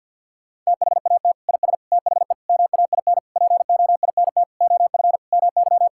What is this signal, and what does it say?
Morse code, heard as a keyed tone, about 51 wpm, 700 Hz. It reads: THAT IS THE MAIN POINT OF MY
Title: Sentences from Top 1000 Words - Review Part 5 - 50wpm